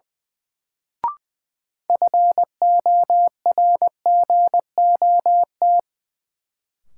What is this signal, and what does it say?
Morse code, heard as a keyed tone, about 20 wpm, 700 Hz. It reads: FORGOT
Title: Sentences from Top 1400 Words - New Word Drill Alt - 20wpm